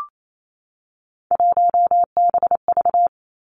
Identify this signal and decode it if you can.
Morse code, heard as a keyed tone, about 28 wpm, 700 Hz. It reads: 1BV
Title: Three Challenging Letter-Number Combinations 28wpm